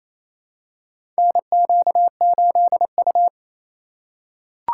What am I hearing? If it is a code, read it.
Morse code, heard as a keyed tone, about 27 wpm, 700 Hz. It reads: NQ8U
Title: USA Call Signs - 3x Character Spacing - 28wpm